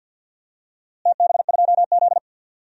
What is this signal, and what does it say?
Morse code, heard as a keyed tone, about 49 wpm, 700 Hz. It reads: TB2Z
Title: DXCC-100 Call Signs - 4 Characters - Repeated 2x - 50wpm